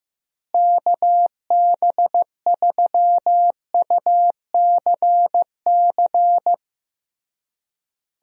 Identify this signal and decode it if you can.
Morse code, heard as a keyed tone, about 15 wpm, 700 Hz. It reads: KB3UCC
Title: USA Call Signs - 6 Characters - Repeated 3x - 15wpm